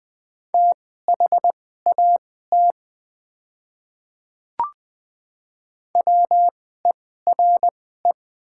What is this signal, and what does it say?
Morse code, heard as a keyed tone, about 20 wpm, spaced as for 14 wpm, 700 Hz. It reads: THAT WERE
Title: 2x Character Spacing - Top 100 Words - 20wpm